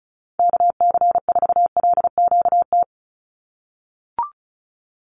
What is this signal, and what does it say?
Morse code, heard as a keyed tone, about 34 wpm, 700 Hz. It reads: KC4LQT
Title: USA Call Signs 35wpm